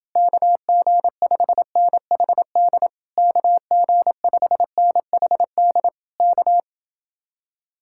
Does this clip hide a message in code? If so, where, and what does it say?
Yes: Morse code from 0.2 to 6.6 s, heard as a keyed tone, about 30 wpm, 700 Hz. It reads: KG5NHD KG5NHD K